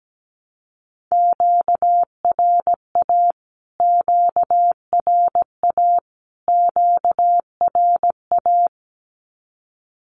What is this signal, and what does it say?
Morse code, heard as a keyed tone, about 17 wpm, 700 Hz. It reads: QRA QRA QRA
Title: Q-Codes Repeated 3x 17wpm